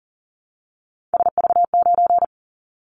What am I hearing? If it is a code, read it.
Morse code, heard as a keyed tone, about 39 wpm, 700 Hz. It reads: SV9
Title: Callsign Prefixes for Top 100 DXCC - 40wpm